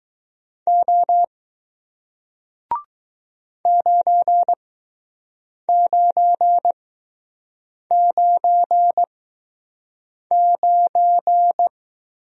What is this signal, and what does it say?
Morse code, heard as a keyed tone, about 20 wpm, spaced as for 6 wpm, 700 Hz. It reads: O 9999